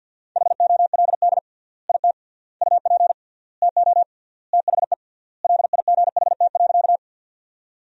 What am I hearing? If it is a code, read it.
Morse code, heard as a keyed tone, about 49 wpm, 700 Hz. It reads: HOLD IT UP TO THE LIGHT.
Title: 3x Word-Spacing - Sentences from Top 400 Words - 50wpm